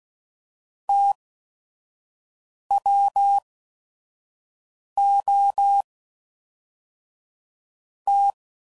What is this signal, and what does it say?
Morse code, heard as a keyed tone, about 16 wpm, spaced as for 5 wpm, 800 Hz. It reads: TWO T
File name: CW Code Groups @5WPM / Technical Subjects / XC